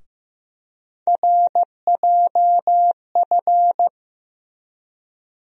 Morse code, heard as a keyed tone, about 15 wpm, 700 Hz. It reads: RJF
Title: USA Call Sign Postfixes - 3 Letters 15wpm